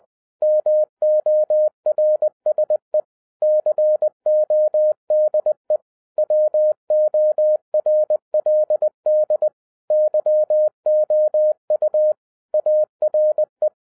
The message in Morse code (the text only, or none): MORSE CODE WORLD YOU ARE